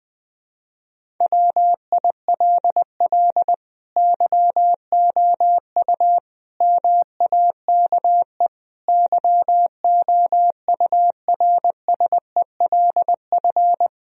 Morse code, heard as a keyed tone, about 20 wpm, 700 Hz. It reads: WILL YOU MAKE YOURSELF
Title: Sentences from Top 1000 Words - Review Part 3 - 20wpm